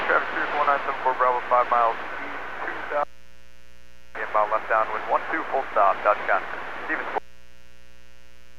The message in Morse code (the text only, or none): none